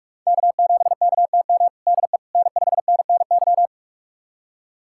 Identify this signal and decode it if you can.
Morse code, heard as a keyed tone, about 46 wpm, 700 Hz. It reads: K7KTM DE N5NNY
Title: USA Callsign DE Callsign 45wpm